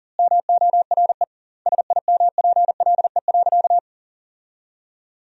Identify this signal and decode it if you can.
Morse code, heard as a keyed tone, about 40 wpm, 700 Hz. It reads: MORE SIMPLE.
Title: 2x Word-Spacing - Sentences from Top 500 Words - 40wpm